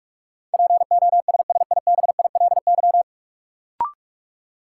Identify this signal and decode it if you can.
Morse code, heard as a keyed tone, about 41 wpm, 700 Hz. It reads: POSSIBILY
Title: Sentences from Top 1300 Words - New Word Drill Alt - 45wpm